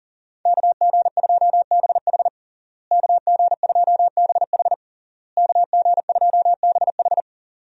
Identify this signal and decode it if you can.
Morse code, heard as a keyed tone, about 40 wpm, 700 Hz. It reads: KG2BH KG2BH KG2BH
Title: USA Call Signs - 5 Characters - Repeated 3x - 40wpm